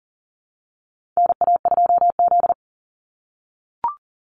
Morse code, heard as a keyed tone, about 40 wpm, 700 Hz. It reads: NA2Z
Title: USA Call Signs 40wpm